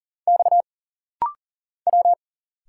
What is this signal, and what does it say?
Morse code, heard as a keyed tone, about 40 wpm, 700 Hz. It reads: X W
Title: Single Letters - Rapid-Fire 40wpm